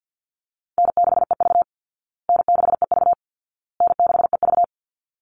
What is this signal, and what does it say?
Morse code, heard as a keyed tone, about 49 wpm, 700 Hz. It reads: N6EV N6EV N6EV
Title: Easy Call Signs Repeated 3x - 50wpm